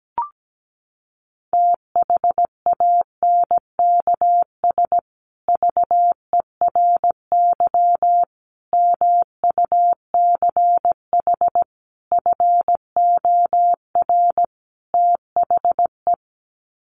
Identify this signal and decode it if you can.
Morse code, heard as a keyed tone, about 17 wpm, 700 Hz. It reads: THANKS VERY MUCH FOR THE
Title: Sets of 5 Words Encore2 17wpm